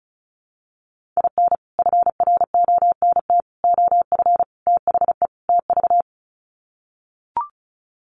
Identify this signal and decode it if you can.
Morse code, heard as a keyed tone, about 34 wpm, 700 Hz. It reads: IN FRONT OF THE TV
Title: Sets of 5 Words Encore2 35wpm